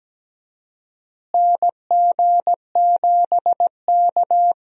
Morse code, heard as a keyed tone, about 17 wpm, 700 Hz. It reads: NG7K
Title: USA Call Signs - 4 Characters - Repeated 2x - 17wpm